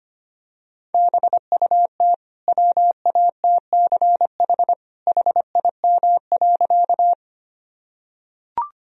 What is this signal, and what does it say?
Morse code, heard as a keyed tone, about 25 wpm, 700 Hz. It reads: BUT WATCH HIM.
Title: Sentences from Top 1000 Words - Review Part 1 - 25wpm